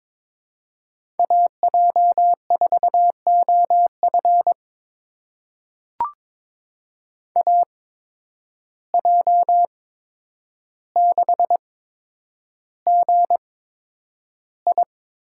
Morse code, heard as a keyed tone, about 22 wpm, spaced as for 9 wpm, 700 Hz. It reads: AJ4OF AJ6GI
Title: USA Call Signs - 5 Characters - 8x Character Spacing - 22wpm